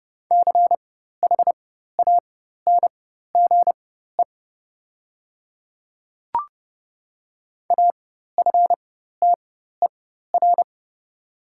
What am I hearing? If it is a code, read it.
Morse code, heard as a keyed tone, about 30 wpm, spaced as for 14 wpm, 700 Hz. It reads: CHANGE AFTER
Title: 4x Character Spacing - Top 300 Words - 30wpm